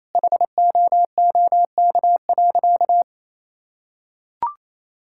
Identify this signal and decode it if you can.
Morse code, heard as a keyed tone, about 28 wpm, 700 Hz. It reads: HOOK.
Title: Sentences from Top 1600 Words 28wpm